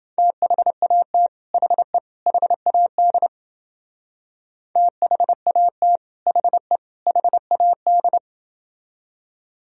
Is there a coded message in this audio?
Yes: Morse code from 0.2 to 8.2 s, heard as a keyed tone, about 31 wpm, 700 Hz. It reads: THAT HE HAD THAT HE HAD